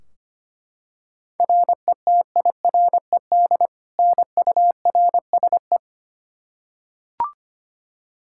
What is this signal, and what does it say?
Morse code, heard as a keyed tone, about 25 wpm, 700 Hz. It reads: RETIRED NURSE
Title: Practice QSO - Exchange 3 - More Info - 25wpm